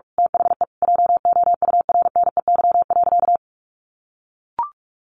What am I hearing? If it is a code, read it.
Morse code, heard as a keyed tone, about 44 wpm, 700 Hz. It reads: THE JOURNEY.
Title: A Sound Of Thunder 45wpm